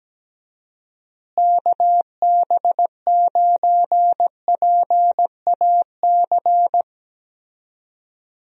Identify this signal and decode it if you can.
Morse code, heard as a keyed tone, about 17 wpm, 700 Hz. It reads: KB9PAC
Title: USA Call Signs - 6 Characters - Repeated 2x - 17wpm